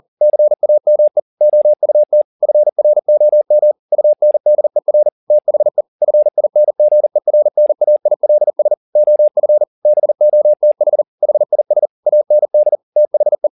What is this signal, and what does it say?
Morse code, heard as a keyed tone, about 40 wpm, 600 Hz. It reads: CAME OUT FROM UNDER THE FINGERNAILS OF BOTH HIS AND THE
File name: the old man and the sea 2 40wpm